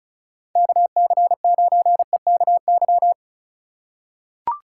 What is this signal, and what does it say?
Morse code, heard as a keyed tone, about 36 wpm, 700 Hz. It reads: KC9EKY
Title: USA Call Signs - 6 Characters - 4x Character Spacing - 35wpm